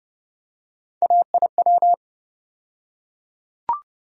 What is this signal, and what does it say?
Morse code, heard as a keyed tone, about 30 wpm, 700 Hz. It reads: AIW
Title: USA Call Sign Postfixes - 3 Letters 30wpm